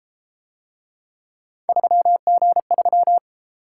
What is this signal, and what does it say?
Morse code, heard as a keyed tone, about 32 wpm, 700 Hz. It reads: 3G3